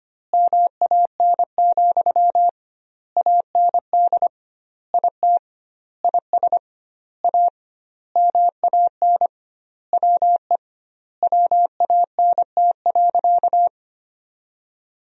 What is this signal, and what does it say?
Morse code, heard as a keyed tone, about 25 wpm, 700 Hz. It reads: MAN, AND IT IS A MAN WE WANT.
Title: 2x Word-Spacing - Sentences from Top 200 Words - 25wpm